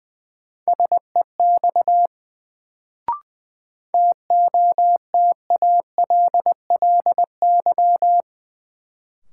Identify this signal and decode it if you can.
Morse code, heard as a keyed tone, about 20 wpm, 700 Hz. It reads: SEX TOTALLY
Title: Sentences from Top 1600 Words - New Word Drill - 20wpm